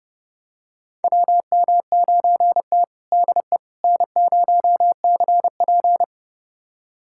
Morse code, heard as a keyed tone, about 30 wpm, 700 Hz. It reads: WM9T DE N0CP
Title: USA Callsign DE Callsign - 4 Characters - 30wpm